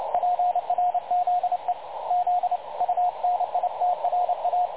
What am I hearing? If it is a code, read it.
Morse code, heard as a keyed tone, about 29 wpm, 720 Hz, with radio noise. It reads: PRZE Z UNITRA
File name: dlazaawansowanych / nagraniazpasma / websdr recording 3565.0kHz